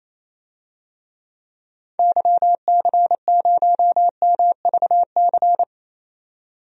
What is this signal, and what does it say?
Morse code, heard as a keyed tone, about 28 wpm, 700 Hz. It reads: YC0MVC
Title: DXCC-100 Call Signs - 6 Characters - 28wpm